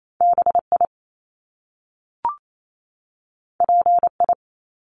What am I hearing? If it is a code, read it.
Morse code, heard as a keyed tone, about 28 wpm, 700 Hz. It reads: BI PI
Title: Top Two Letter Combinations in Words 28wpm